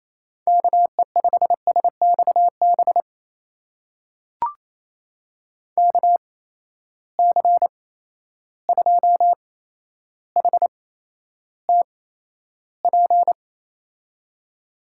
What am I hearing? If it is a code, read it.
Morse code, heard as a keyed tone, about 28 wpm, spaced as for 12 wpm, 700 Hz. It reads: KE5SXB KC2HTP
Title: USA Call Signs - 6 Characters - 8x Character Spacing - 28wpm